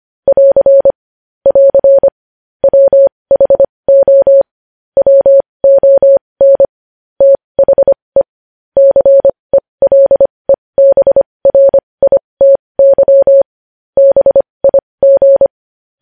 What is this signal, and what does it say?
Morse code, heard as a keyed tone, about 25 wpm, spaced as for 20 wpm, 550 Hz. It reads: <AR> <AR> WHO WON THE CELEBRITY BIG
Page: news headlines in morse code at 20 wpm